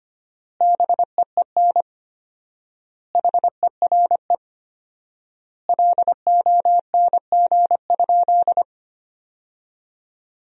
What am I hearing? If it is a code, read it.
Morse code, heard as a keyed tone, about 25 wpm, 700 Hz. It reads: BEEN HERE LONG?
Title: 4x Word-Spacing - Sentences from Top 200 Words - 25wpm